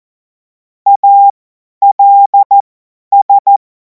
Morse code, heard as a keyed tone, about 14 wpm, spaced as for 10 wpm, 800 Hz. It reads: ALS
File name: Morsekurs / Witze / Witz 23. Juni 2026 50 F70